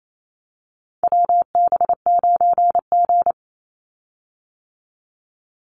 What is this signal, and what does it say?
Morse code, heard as a keyed tone, about 28 wpm, 700 Hz. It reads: WB9G
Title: Call Signs - 28wpm